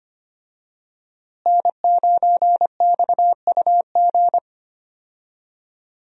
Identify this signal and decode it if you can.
Morse code, heard as a keyed tone, about 25 wpm, 700 Hz. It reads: N9XUG
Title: USA Call Signs - 5 Characters - Repeated 3x - 25wpm